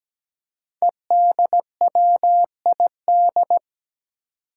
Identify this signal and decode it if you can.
Morse code, heard as a keyed tone, about 17 wpm, 700 Hz. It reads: EDWID
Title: Five Challenging Letter-Number Combinations 17wpm